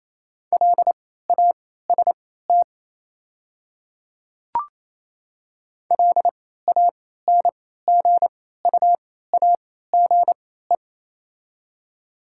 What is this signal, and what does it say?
Morse code, heard as a keyed tone, about 28 wpm, spaced as for 16 wpm, 700 Hz. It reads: LAST LANGUAGE
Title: 3x Character Spacing - Top 500 Words - 28wpm